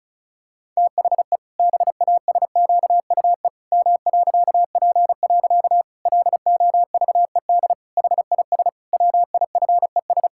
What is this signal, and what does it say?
Morse code, heard as a keyed tone, about 36 wpm, 700 Hz. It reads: THE BASQUE M.P. LOVED HIS WIFES
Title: Pangram Challenge v2 - 35wpm